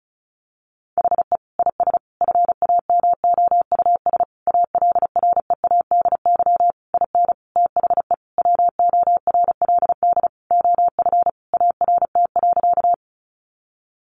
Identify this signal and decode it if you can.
Morse code, heard as a keyed tone, about 34 wpm, 700 Hz. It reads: HE IS FAMOUS ALREADY IN THE WORLD OF ART.
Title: Sentences from Top 1500 Words 35wpm